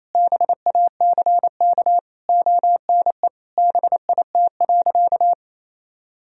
Morse code, heard as a keyed tone, about 28 wpm, 700 Hz. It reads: BACK ONE BIT.
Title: Sentences from Top 1000 Words - Review Part 1 - 28wpm